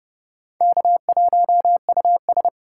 Morse code, heard as a keyed tone, about 29 wpm, 700 Hz. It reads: K1US